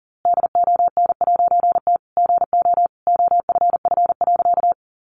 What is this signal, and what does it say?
Morse code, heard as a keyed tone, about 40 wpm, 700 Hz. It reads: DON'T GO OFF.
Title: A Sound Of Thunder 40wpm